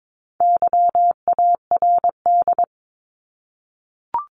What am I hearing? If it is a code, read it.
Morse code, heard as a keyed tone, about 22 wpm, 700 Hz. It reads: YARD
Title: Top 1751-2000 Words 22wpm